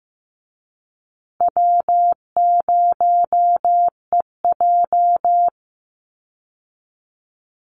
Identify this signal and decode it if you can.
Morse code, heard as a keyed tone, about 14 wpm, 700 Hz. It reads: W0EJ
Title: Easy Call Signs Repeated 3x - 15wpm